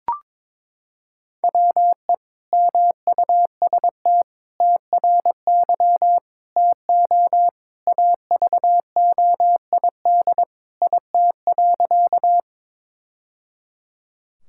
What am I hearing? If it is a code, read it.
Morse code, heard as a keyed tone, about 22 wpm, 700 Hz. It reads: WE MUST TRY TO AVOID IT.
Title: Sentences from Top 1000 Words - Review Part 6 - 22wpm